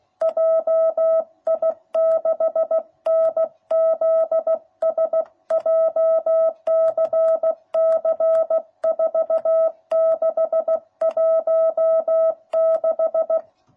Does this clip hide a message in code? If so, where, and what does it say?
Yes: Morse code from 0.2 to 13.4 s, heard as a keyed tone, about 16 wpm, 660 Hz. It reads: JI6NZS JCC 4616